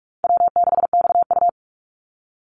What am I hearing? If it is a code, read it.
Morse code, heard as a keyed tone, about 44 wpm, 700 Hz. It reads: W6XU
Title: Call Signs Repeated 3x - 45wpm